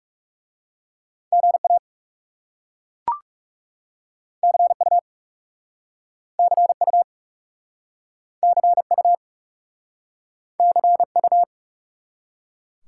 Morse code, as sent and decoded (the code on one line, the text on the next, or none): --. .- / -.-. ..- / -.-. ..- / -.-. ..- / -.-. ..-
GA CU CU CU CU